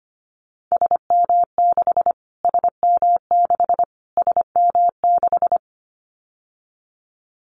Morse code, heard as a keyed tone, about 25 wpm, 700 Hz. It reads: SM6 SM6 SM6
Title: Callsign Prefixes for Top 25 DXCC - Repeated 3x - 25wpm